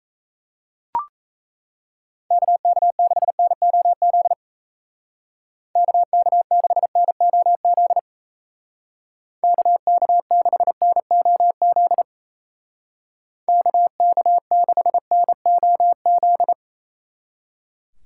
Morse code, heard as a keyed tone, about 37 wpm, 700 Hz. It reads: KK6NOZ KK6NOZ KK6NOZ KK6NOZ